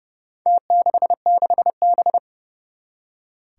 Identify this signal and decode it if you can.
Morse code, heard as a keyed tone, about 32 wpm, 700 Hz. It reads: T66B
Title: Four Challenging Letter-Number Combinations 30wpm